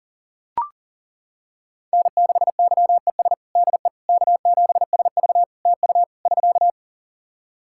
Morse code, heard as a keyed tone, about 41 wpm, 700 Hz. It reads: N6YES DE K7SV TU <SK>
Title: K1USN SST QSO Practice 40wpm